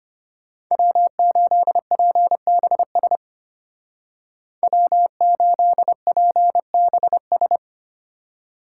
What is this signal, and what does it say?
Morse code, heard as a keyed tone, about 33 wpm, 700 Hz. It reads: W8PBS W8PBS